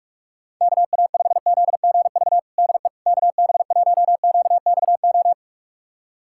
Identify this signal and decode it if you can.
Morse code, heard as a keyed tone, about 46 wpm, 700 Hz. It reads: KA5ZGV DE KB1QXO
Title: USA Callsign DE Callsign - 6 Characters - 45wpm